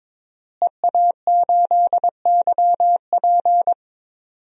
Morse code, heard as a keyed tone, about 21 wpm, 700 Hz. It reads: EA8YP
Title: DXCC-100 Call Signs - 5 Characters - Repeated 3x - 22wpm